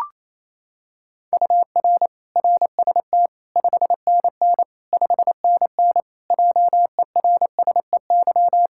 Morse code, heard as a keyed tone, about 28 wpm, 700 Hz. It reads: UR RST 5NN 5NN JERSEY
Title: Practice QSO - Exchange 1 - Beginning - 28wpm